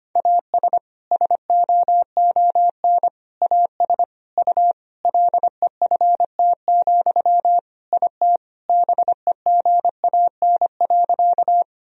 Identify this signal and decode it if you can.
Morse code, heard as a keyed tone, about 25 wpm, 700 Hz. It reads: AS SOON AS U LEFT, IT BEGAN.
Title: Sentences from Top 400 Words and Abbreviations 25wpm